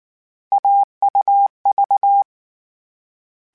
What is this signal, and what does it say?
Morse code, heard as a keyed tone, about 19 wpm, 800 Hz. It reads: AUV